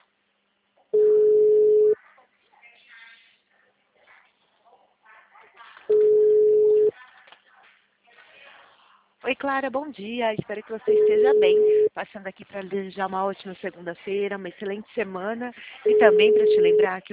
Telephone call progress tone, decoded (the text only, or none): ringback tone